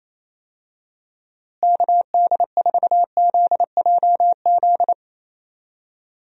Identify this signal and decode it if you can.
Morse code, heard as a keyed tone, about 28 wpm, 700 Hz. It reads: KD4ZJZ
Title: USA Call Signs - 6 Characters - Repeated 2x - 28wpm